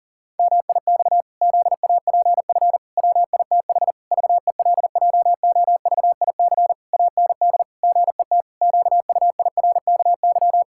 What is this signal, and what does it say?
Morse code, heard as a keyed tone, about 40 wpm, 700 Hz. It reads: MIX ZAPF WITH VELJOVIC AND GET QUIRKY
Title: Pangram Challenge v2 - 40wpm